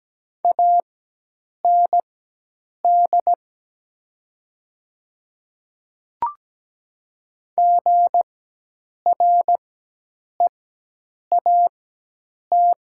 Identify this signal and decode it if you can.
Morse code, heard as a keyed tone, about 17 wpm, spaced as for 8 wpm, 700 Hz. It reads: AND GREAT
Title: 4x Character Spacing - Top 200 Words - 17wpm